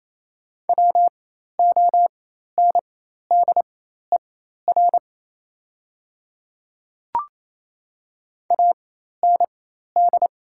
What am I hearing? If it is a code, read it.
Morse code, heard as a keyed tone, about 27 wpm, spaced as for 13 wpm, 700 Hz. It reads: WONDER AND
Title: 4x Character Spacing - Top 500 Words - 28wpm